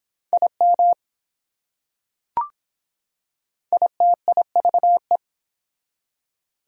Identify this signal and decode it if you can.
Morse code, heard as a keyed tone, about 24 wpm, 700 Hz. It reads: IM ITIVE